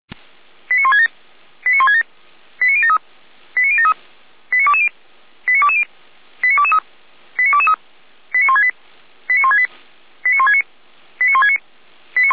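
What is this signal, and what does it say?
Five-tone selective call (ZVEI), heard as quick runs of five tones, tones of about 80 ms. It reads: 89167 89167 89073 89073 89220 89220 89223 89223 89168 89168 89169 89169
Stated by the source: ZVEI five-tone selective call received on an SDR.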